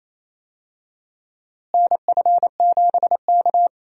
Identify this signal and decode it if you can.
Morse code, heard as a keyed tone, about 28 wpm, 700 Hz. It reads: NF7K
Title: USA Call Signs - 4 Characters - Repeated 2x - 28wpm